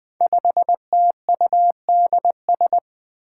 Morse code, heard as a keyed tone, about 20 wpm, 700 Hz. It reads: 5TUDS
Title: Five Challenging Letter-Number Combinations 20wpm